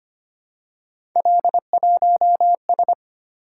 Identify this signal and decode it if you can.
Morse code, heard as a keyed tone, about 25 wpm, 700 Hz. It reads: L1S
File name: Three Letter-Number Combinations - 25wpm